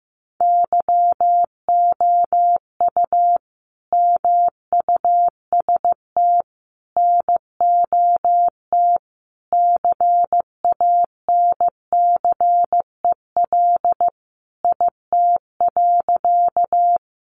Morse code, heard as a keyed tone, about 15 wpm, 700 Hz. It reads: YOU MUST NOT CANCEL IT.
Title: Sentences from Top 1200 Words 15wpm